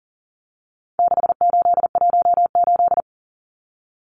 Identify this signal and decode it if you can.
Morse code, heard as a keyed tone, about 40 wpm, 700 Hz. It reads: 6818
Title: Numbers - 4-Digit 40wpm